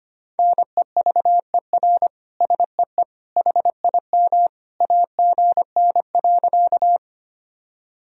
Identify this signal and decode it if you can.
Morse code, heard as a keyed tone, about 25 wpm, 700 Hz. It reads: NEVER SEE HIM AGN.
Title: Sentences from Top 300 Words and Abbreviations 25wpm